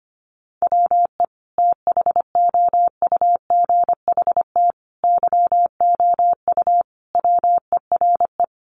Morse code, heard as a keyed tone, about 25 wpm, 700 Hz. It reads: WE THOUGHT YOU WERE
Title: Sentences from Top 800 Words 25wpm